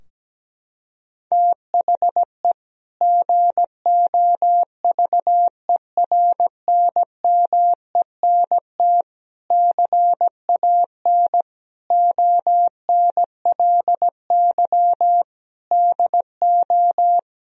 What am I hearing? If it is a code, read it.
Morse code, heard as a keyed tone, about 17 wpm, 700 Hz. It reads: THE GOVERNMENT CAN ONLY DO
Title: Sentences from Top 1000 Words - Review Part 3 - 17wpm